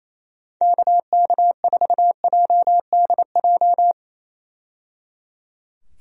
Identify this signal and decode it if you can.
Morse code, heard as a keyed tone, about 27 wpm, 700 Hz. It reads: KK4JDJ
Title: USA Call Signs - 6 Characters - 28wpm